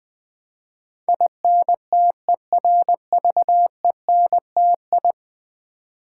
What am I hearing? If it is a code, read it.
Morse code, heard as a keyed tone, about 20 wpm, 700 Hz. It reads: INTERVENTI
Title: Build A Word - 5 Letters At A Time - 12-Letter Words - 20wpm